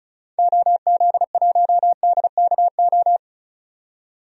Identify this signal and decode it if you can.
Morse code, heard as a keyed tone, about 36 wpm, 700 Hz. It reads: OZ1DKO
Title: DXCC-100 Call Signs - 6 Characters - 35wpm